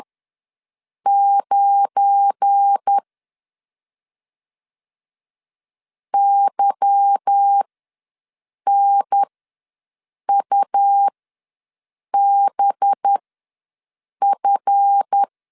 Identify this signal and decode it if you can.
Morse code, heard as a keyed tone, about 10 wpm, spaced as for 6 wpm, 800 Hz. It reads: 9 YNUBF